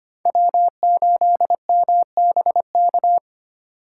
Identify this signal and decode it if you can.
Morse code, heard as a keyed tone, about 32 wpm, 700 Hz. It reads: W8MBK